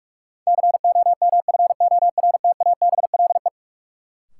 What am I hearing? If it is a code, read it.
Morse code, heard as a keyed tone, about 46 wpm, 700 Hz. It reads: COMFORTABLE
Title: Sentences from Top 1000 Words - Review Part 7 - New Word Drill - 45wpm